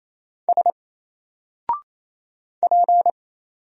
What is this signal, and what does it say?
Morse code, heard as a keyed tone, about 27 wpm, 700 Hz. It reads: S P
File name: Single-Letter-Number-28wpm